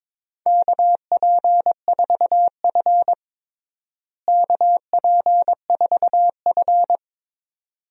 Morse code, heard as a keyed tone, about 22 wpm, 700 Hz. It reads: KP4F KP4F
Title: DXCC-100 Call Signs - 4 Characters - Repeated 2x - 22wpm